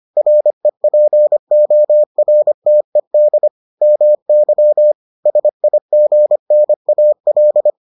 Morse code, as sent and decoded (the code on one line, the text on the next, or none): .-. .--. --- .-. - . -.. / -- -.-- / ... .. --. -. .- .-..
REPORTED MY SIGNAL